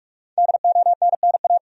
Morse code, heard as a keyed tone, about 46 wpm, 700 Hz. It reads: DONNA
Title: USA Common Names - Max 5 Letters - 45wpm